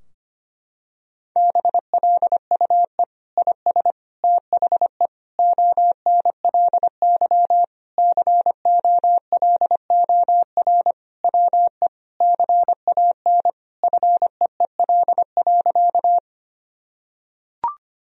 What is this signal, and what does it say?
Morse code, heard as a keyed tone, about 25 wpm, 700 Hz. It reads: BLUE IS THE ONLY COLOR WE CAN FEEL.
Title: Sentences from Top 1000 Words - Review Part 3 - 25wpm